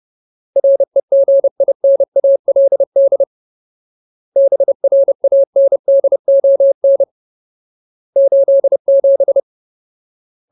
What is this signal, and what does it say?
Morse code, heard as a keyed tone, about 30 wpm, 550 Hz. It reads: REGINALD BRANDON 87
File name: Ultimate Word List / 30WPM Ultimate